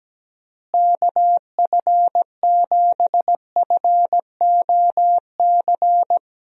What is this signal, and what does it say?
Morse code, heard as a keyed tone, about 17 wpm, 700 Hz. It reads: KF7FOC
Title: USA Call Signs - 6 Characters - Repeated 2x - 17wpm